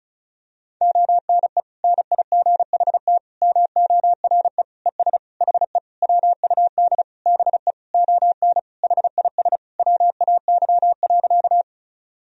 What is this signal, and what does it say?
Morse code, heard as a keyed tone, about 36 wpm, 700 Hz. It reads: ONE NIGHT MORE ES HE WUD BE ON HIS WAY.
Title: Sentences from Top 300 Words and Abbreviations 35wpm